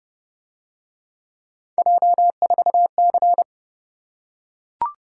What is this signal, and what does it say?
Morse code, heard as a keyed tone, about 29 wpm, 700 Hz. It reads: J4C